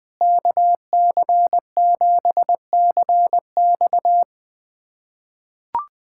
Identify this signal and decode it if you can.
Morse code, heard as a keyed tone, about 20 wpm, 700 Hz. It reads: KC7CX
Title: USA Callsign DE Callsign - 5 Characters - 20wpm